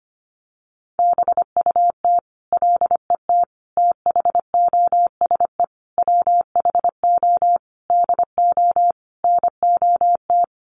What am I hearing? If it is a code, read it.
Morse code, heard as a keyed tone, about 25 wpm, 700 Hz. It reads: BUT LET THOSE WHO DO NOT
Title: Sentences from Top 1500 Words 25wpm